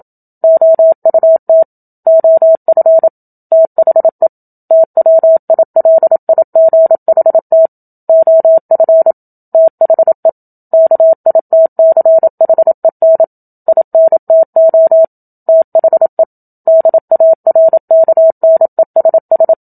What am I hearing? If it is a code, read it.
Morse code, heard as a keyed tone, about 28 wpm, 650 Hz. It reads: OUT OF THE TWILIGHT OF THE KITCHEN INTO THE DARKNESS
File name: Book Two / 01-Chapter 1 - 27 WPM